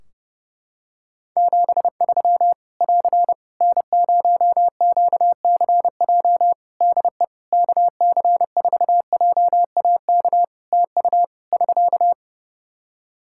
Morse code, heard as a keyed tone, about 30 wpm, 700 Hz. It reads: 73 <AR> N0QCJ DE KC4JAK TU <SK>
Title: K1USN SST QSO Practice 30wpm